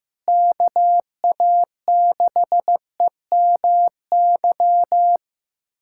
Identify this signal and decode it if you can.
Morse code, heard as a keyed tone, about 15 wpm, 700 Hz. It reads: KA6EMY
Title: USA Callsign DE Callsign - 6 Characters - 15wpm